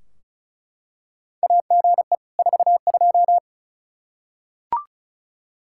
Morse code, heard as a keyed tone, about 36 wpm, 700 Hz. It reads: AGE 42
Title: Practice QSO - Exchange 3 - More Info - 35wpm